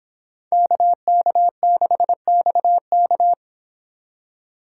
Morse code, heard as a keyed tone, about 24 wpm, 700 Hz. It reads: KK6XK